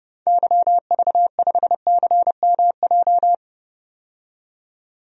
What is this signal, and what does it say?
Morse code, heard as a keyed tone, about 32 wpm, 700 Hz. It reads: YV5CMJ
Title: DXCC-100 Call Signs - 6 Characters - 30wpm